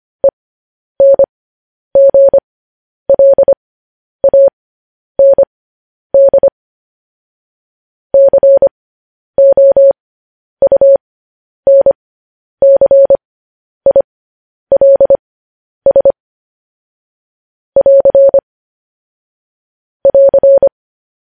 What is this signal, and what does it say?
Morse code, heard as a keyed tone, about 25 wpm, spaced as for 10 wpm, 550 Hz. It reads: ENGLAND COUNCILS <AR> <AR>
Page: news-headlines-in-morse-code-at-10-wpm